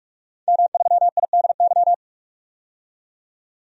Morse code, heard as a keyed tone, about 46 wpm, 700 Hz. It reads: M3IDY
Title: DXCC-100 Call Signs - 5 Characters - 45wpm